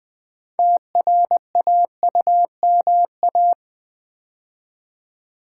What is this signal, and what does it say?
Morse code, heard as a keyed tone, about 20 wpm, 690 Hz. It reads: TRAUMA
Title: 6 Letter Words - First 3 Letters Given - 20wpm